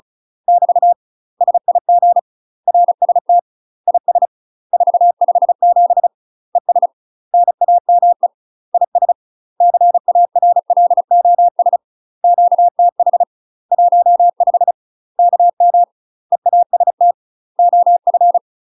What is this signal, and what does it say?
Morse code, heard as a keyed tone, about 34 wpm, 700 Hz. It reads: <BT> SIG RST IS 457 ES NAME IS CARLOS QTH 15 KM EAST OF
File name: UK MASTER QSOs TRAINING / 35WPM / UK MASTER QSOs 0039